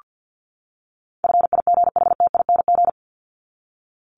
Morse code, heard as a keyed tone, about 50 wpm, 700 Hz. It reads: FIGHTING
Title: ING Suffix - Top 100 Words with Base Words 50wpm